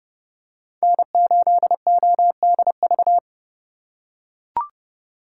Morse code, heard as a keyed tone, about 32 wpm, 700 Hz. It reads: N8ODV
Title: USA Call Signs - 4x Character Spacing - 30wpm